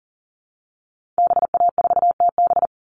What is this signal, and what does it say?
Morse code, heard as a keyed tone, about 40 wpm, 700 Hz. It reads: BA4TB
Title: Call Signs Repeated 3x - 40wpm